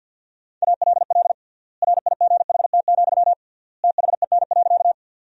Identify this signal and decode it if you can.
Morse code, heard as a keyed tone, about 50 wpm, 700 Hz. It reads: ALL RIGHT, THEN.
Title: 3x Word-Spacing - Sentences from Top 200 Words - 50wpm